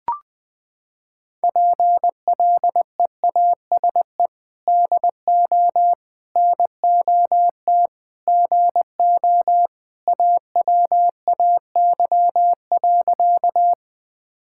Morse code, heard as a keyed tone, about 21 wpm, 700 Hz. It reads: PLEASE DO NOT GO AWAY.
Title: Sentences with 4-Word Phrases - 20wpm